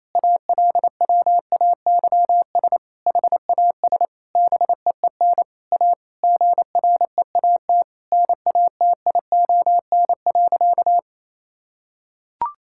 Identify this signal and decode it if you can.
Morse code, heard as a keyed tone, about 28 wpm, 700 Hz. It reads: ALWAYS HAS BEEN A GREAT NATION.
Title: Sentences from Top 1000 Words - Review Part 6 - 28wpm